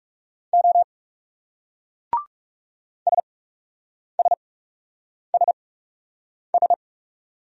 Morse code, heard as a keyed tone, about 38 wpm, spaced as for 8 wpm, 700 Hz. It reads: O SSSS